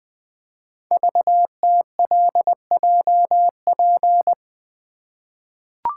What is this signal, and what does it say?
Morse code, heard as a keyed tone, about 20 wpm, 700 Hz. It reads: VTLJP
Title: Code Groups - 5 Letters - 20wpm